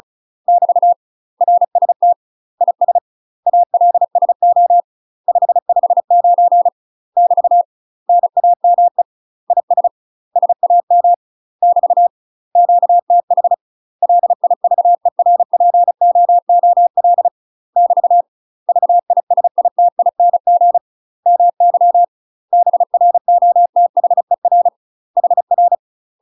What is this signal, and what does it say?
Morse code, heard as a keyed tone, about 34 wpm, 700 Hz. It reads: <BT> RST IS ALSO 559 <BT> NAME IS SAM <BT> QTH LIVERPOOL <BT> VISITING MY BROTHER HR
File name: UK MASTER QSOs TRAINING / 35WPM / UK MASTER QSOs 0040